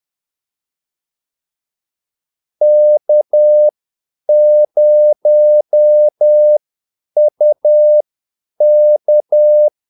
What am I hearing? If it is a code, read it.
Morse code, heard as a keyed tone, about 10 wpm, spaced as for 8 wpm, 600 Hz. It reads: K0UK
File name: Callsigns-10-8wpm